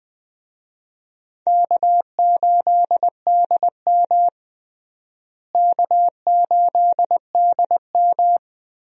Morse code, heard as a keyed tone, about 20 wpm, 700 Hz. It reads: K8DM K8DM
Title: USA Call Signs - 4 Characters - Repeated 2x - 20wpm